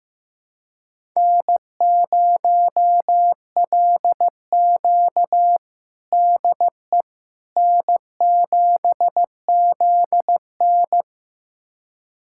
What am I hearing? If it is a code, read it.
Morse code, heard as a keyed tone, about 16 wpm, 700 Hz. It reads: N0LQ DE N7ZN
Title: USA Callsign DE Callsign - 4 Characters - 15wpm